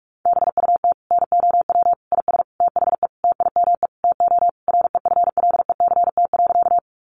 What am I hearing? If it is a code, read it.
Morse code, heard as a keyed tone, about 44 wpm, 700 Hz. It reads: BUT NOW IS THE TIME TO REFLECT.
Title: Sentences from Top 1500 Words 45wpm